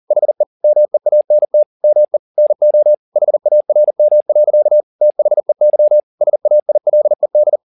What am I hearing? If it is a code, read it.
Morse code, heard as a keyed tone, about 40 wpm, 600 Hz. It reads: HE MEANT ME NO HARM. THEY SAILED
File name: the old man and the sea 2 40wpm